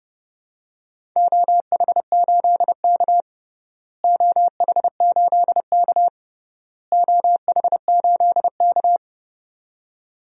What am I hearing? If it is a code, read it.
Morse code, heard as a keyed tone, about 30 wpm, 700 Hz. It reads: OH8K OH8K OH8K
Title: DXCC-100 Call Signs - 4 Characters - Repeated 3x - 30wpm